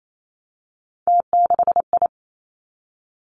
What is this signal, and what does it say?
Morse code, heard as a keyed tone, about 28 wpm, 700 Hz. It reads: T6I
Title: Three Challenging Letter-Number Combinations 28wpm